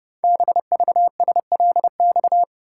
Morse code, heard as a keyed tone, about 30 wpm, 700 Hz. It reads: BVSLX
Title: Code Groups - 5 Letter:Numbers - 30wpm